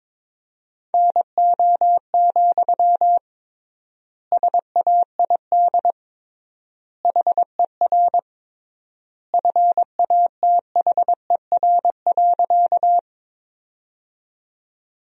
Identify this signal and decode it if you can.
Morse code, heard as a keyed tone, about 21 wpm, 700 Hz. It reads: NO, SAID HER FATHER.
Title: 3x Word-Spacing - Sentences from Top 300 Words - 22wpm